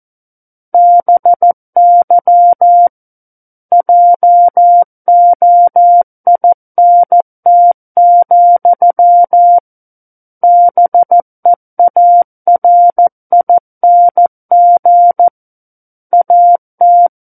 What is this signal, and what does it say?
Morse code, heard as a keyed tone, about 14 wpm, 700 Hz. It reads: BY JOINT, BEARING AT
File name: Book One / 05-Chapter 5 - 14 WPM